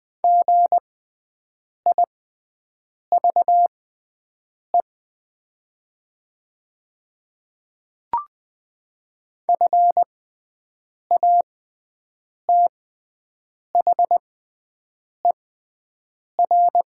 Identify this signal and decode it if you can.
Morse code, heard as a keyed tone, about 20 wpm, spaced as for 7 wpm, 700 Hz. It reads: GIVE FATHER
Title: 6x Character Spacing - Top 200 Words - 20wpm